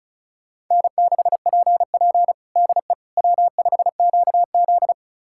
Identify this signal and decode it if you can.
Morse code, heard as a keyed tone, about 34 wpm, 700 Hz. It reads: N6PP DE W5QZ
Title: USA Callsign DE Callsign - 4 Characters - 35wpm